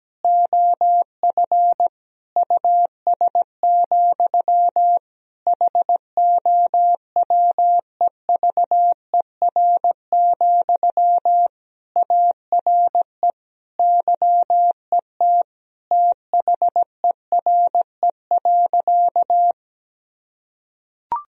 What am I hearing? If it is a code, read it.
Morse code, heard as a keyed tone, about 17 wpm, 700 Hz. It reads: OF US, HOWEVER, ARE YET THERE.
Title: Sentences from Top 1000 Words - Review Part 1 - 17wpm